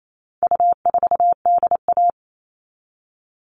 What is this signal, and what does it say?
Morse code, heard as a keyed tone, about 28 wpm, 700 Hz. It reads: U4DA
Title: All Cut Numbers - 4-Digit 28wpm